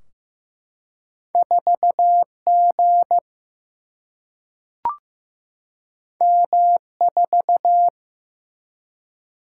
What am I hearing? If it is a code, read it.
Morse code, heard as a keyed tone, about 15 wpm, 700 Hz. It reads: 4G M4
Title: Single-Letter Single-Number 15wpm